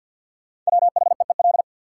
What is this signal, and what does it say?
Morse code, heard as a keyed tone, about 53 wpm, 700 Hz. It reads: WHEEL
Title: Sentences from Top 500 Words - New Word Drill - 50wpm